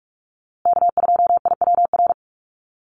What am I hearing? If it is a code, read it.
Morse code, heard as a keyed tone, about 46 wpm, 690 Hz. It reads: K2IWR
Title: USA Call Signs Repeated 3x 45wpm